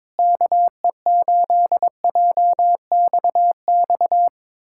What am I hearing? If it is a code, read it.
Morse code, heard as a keyed tone, about 22 wpm, 700 Hz. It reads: KE8JXX
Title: USA Callsign DE Callsign - 6 Characters - 22wpm